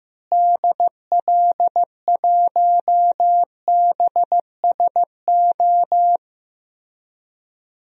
Morse code, heard as a keyed tone, about 15 wpm, 700 Hz. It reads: DL1BSO
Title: DXCC-100 Call Signs - 6 Characters - Repeated 2x - 15wpm